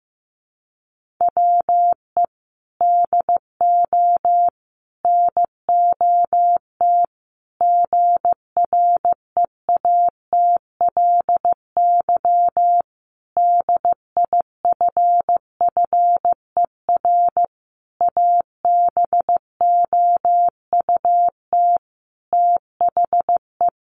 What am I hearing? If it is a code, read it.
Morse code, heard as a keyed tone, about 15 wpm, 700 Hz. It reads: WE DO NOT GREATLY DIFFER ABOUT THE
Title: Sentences from Top 1400 Words 15wpm